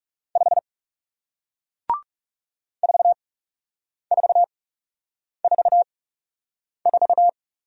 Morse code, heard as a keyed tone, about 40 wpm, spaced as for 8 wpm, 700 Hz. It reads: 5 4444